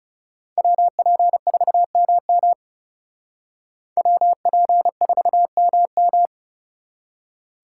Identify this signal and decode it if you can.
Morse code, heard as a keyed tone, about 36 wpm, 700 Hz. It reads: WP4MM WP4MM
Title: Speed Racing - USA 2x2 Call Signs 30wpm